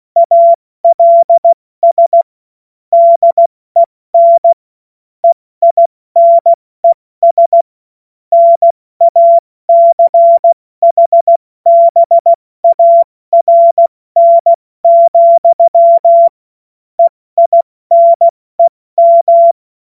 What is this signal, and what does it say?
Morse code, heard as a keyed tone, about 16 wpm, 680 Hz. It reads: ALS DEN EINES NACHBARN, EINEM
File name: Morsekurs / Witze / Witz 04. Oktober 2022 70 F80